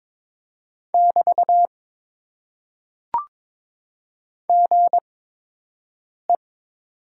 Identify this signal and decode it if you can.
Morse code, heard as a keyed tone, about 22 wpm, spaced as for 6 wpm, 700 Hz. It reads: <BT> GE
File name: Top 90 QSO Elements - 8x Character Spacing - 22wpm